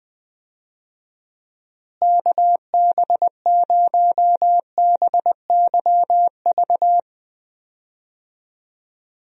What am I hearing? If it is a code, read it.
Morse code, heard as a keyed tone, about 20 wpm, 700 Hz. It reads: KB0BYV
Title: USA Call Signs - 6 Characters - Repeated 3x - 20wpm